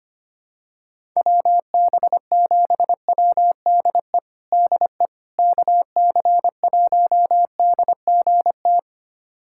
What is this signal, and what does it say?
Morse code, heard as a keyed tone, about 25 wpm, 700 Hz. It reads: WB7WDE DE KC1DGT
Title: USA Callsign DE Callsign - 6 Characters - 25wpm